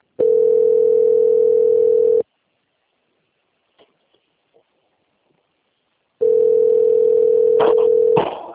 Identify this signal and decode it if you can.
Ringback tone (North America), 440 + 480 Hz.
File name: out- - - - - .68